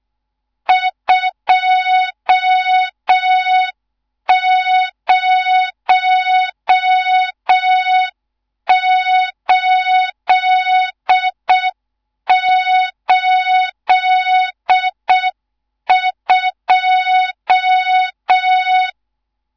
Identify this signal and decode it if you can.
Morse code, heard as a keyed tone, about 6 wpm, 3700 Hz, with noise. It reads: 20882
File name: Morze-HA2MN / hang / 14 280 HA2MN